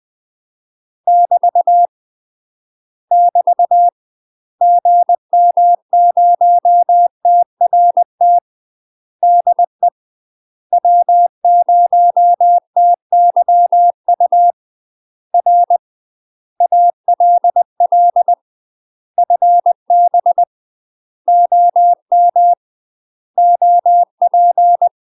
Morse code, heard as a keyed tone, about 20 wpm, 700 Hz. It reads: <BT> <BT> GM0TRT DE W0TYU R ALL FB OM OP
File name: UK MASTER QSOs TRAINING / 20WPM / UK MASTER QSOs 0013